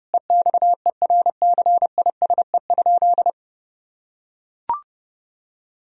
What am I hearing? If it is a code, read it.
Morse code, heard as a keyed tone, about 32 wpm, 700 Hz. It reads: EXERCISE?
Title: Sentences from Top 1000 Words - Review Part 4 - 30wpm